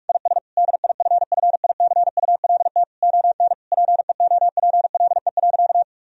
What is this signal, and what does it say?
Morse code, heard as a keyed tone, about 41 wpm, 700 Hz. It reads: IS DIFFICULT ON PEOPLE.
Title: Sentences from Top 1000 Words - Review Part 3 - 45wpm